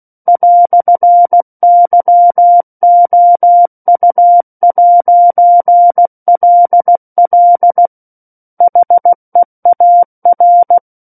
Morse code, heard as a keyed tone, about 16 wpm, 700 Hz. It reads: "YOU'LL HEAR
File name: Book One / 07-Chapter 7 - 16 WPM